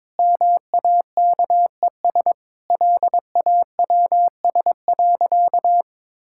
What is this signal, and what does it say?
Morse code, heard as a keyed tone, about 22 wpm, 700 Hz. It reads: MAKES LAWS.
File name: practice / 22wpm / Sentences with 2-Word Phrases - 22wpm